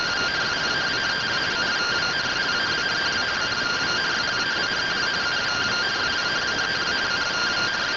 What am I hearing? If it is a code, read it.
Two-tone teleprinter signal, about 170 Hz shift.